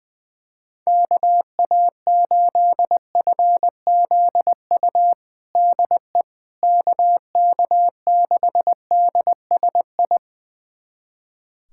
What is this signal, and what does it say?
Morse code, heard as a keyed tone, about 21 wpm, 700 Hz. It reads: KA8FZU DE KK6DSI
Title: USA Callsign DE Callsign - 6 Characters - 20wpm